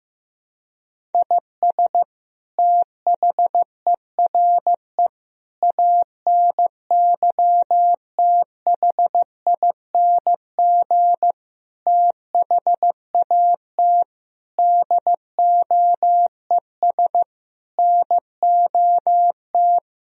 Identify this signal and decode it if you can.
Morse code, heard as a keyed tone, about 15 wpm, 700 Hz. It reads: IS THERE ANYTHING THAT DOES NOT
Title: Sentences from Top 1000 Words - Review Part 1 - 15wpm